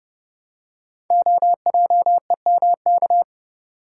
Morse code, heard as a keyed tone, about 30 wpm, 700 Hz. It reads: OJEMK
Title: Code Groups - 5 Letters - 30wpm